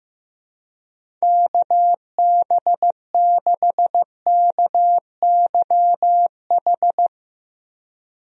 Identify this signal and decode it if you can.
Morse code, heard as a keyed tone, about 15 wpm, 700 Hz. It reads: KB6KYH
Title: USA Call Signs - 6 Characters - Repeated 2x - 15wpm